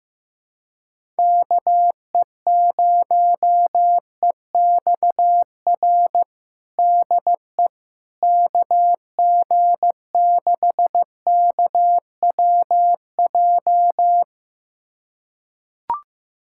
Morse code, heard as a keyed tone, about 15 wpm, 700 Hz. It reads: KE0EXR DE KG6KWJ
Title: USA Callsign DE Callsign - 6 Characters - 15wpm